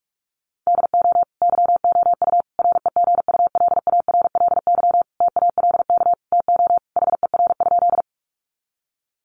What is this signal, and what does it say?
Morse code, heard as a keyed tone, about 44 wpm, 700 Hz. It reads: DO YOU REGULARLY TALK TO HER?
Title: Sentences from Top 1400 Words 45wpm